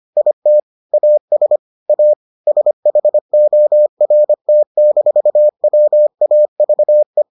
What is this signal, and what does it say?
Morse code, heard as a keyed tone, about 25 wpm, 600 Hz. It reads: IT AS A SHORT-WAVE